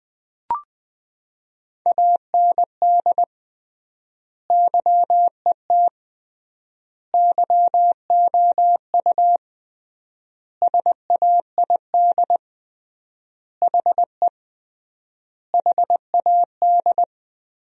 Morse code, heard as a keyed tone, about 20 wpm, 700 Hz. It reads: AND YET YOU SAID HE HAD
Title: 3x Word-Spacing - Sentences from Top 500 Words - 20wpm